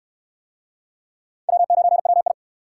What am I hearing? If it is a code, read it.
Morse code, heard as a keyed tone, about 68 wpm, 700 Hz. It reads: W9WI